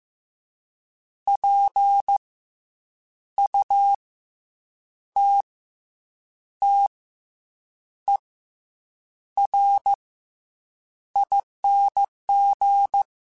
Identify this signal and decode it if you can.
Morse code, heard as a keyed tone, about 15 wpm, 800 Hz. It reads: P U T T E R ING
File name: Special INGs / Special INGs-mp3s / SlowToFaster ing @13